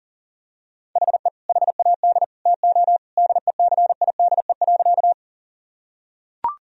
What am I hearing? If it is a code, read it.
Morse code, heard as a keyed tone, about 40 wpm, 700 Hz. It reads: HE HAD TO DECIDE.
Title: Sentences from Top 1000 Words - Review Part 3 - 40wpm